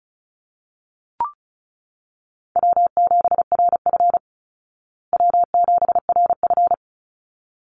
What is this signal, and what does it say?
Morse code, heard as a keyed tone, about 34 wpm, 700 Hz. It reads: W7RF W7RF
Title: Easy Call Signs Repeated 2x - 35wpm